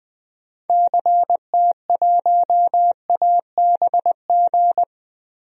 Morse code, heard as a keyed tone, about 21 wpm, 700 Hz. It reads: CT1ABG
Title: DXCC-100 Call Signs - 6 Characters - Repeated 2x - 20wpm